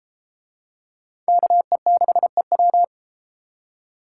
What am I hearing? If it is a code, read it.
Morse code, heard as a keyed tone, about 30 wpm, 700 Hz. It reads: KE6EW